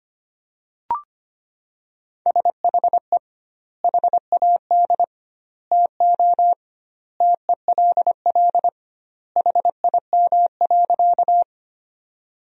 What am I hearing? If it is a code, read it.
Morse code, heard as a keyed tone, about 25 wpm, 700 Hz. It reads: SHE HAD TO TELL HIM.
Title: 2x Word-Spacing - Sentences from Top 200 Words - 25wpm